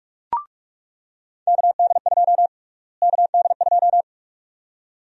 Morse code, heard as a keyed tone, about 42 wpm, 700 Hz. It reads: KD2 KD2
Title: USA Call Sign Prefixes - Repeated 2x - 45wpm